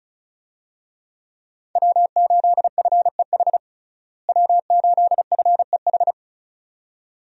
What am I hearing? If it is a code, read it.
Morse code, heard as a keyed tone, about 34 wpm, 700 Hz. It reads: W8FEH W8FEH
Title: USA Call Signs - 5 Characters - Repeated 2x - 35wpm